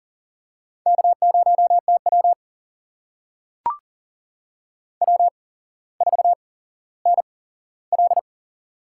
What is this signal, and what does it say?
Morse code, heard as a keyed tone, about 40 wpm, spaced as for 17 wpm, 700 Hz. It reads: K0TW W4NL
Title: USA Call Signs - 4 Characters - 8x Character Spacing - 40wpm